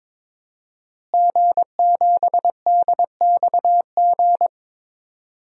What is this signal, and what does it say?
Morse code, heard as a keyed tone, about 21 wpm, 700 Hz. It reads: G7DXG
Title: DXCC-100 Call Signs - 5 Characters - Repeated 3x - 22wpm